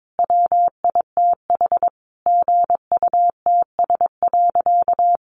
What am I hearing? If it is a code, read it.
Morse code, heard as a keyed tone, about 22 wpm, 700 Hz. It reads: WITH GUTS.
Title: A Sound Of Thunder 22wpm